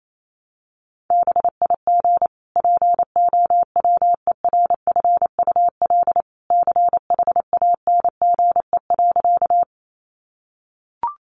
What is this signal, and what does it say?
Morse code, heard as a keyed tone, about 28 wpm, 700 Hz. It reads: BIG POWERFUL CHANGE.
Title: Sentences from Top 1300 Words 28wpm